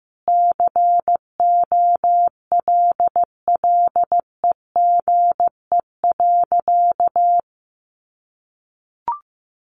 Morse code, heard as a keyed tone, about 15 wpm, 700 Hz. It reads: COLLEGE.
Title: Sentences from Top 1300 Words 15wpm